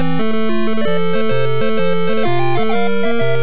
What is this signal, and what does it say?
There is no code.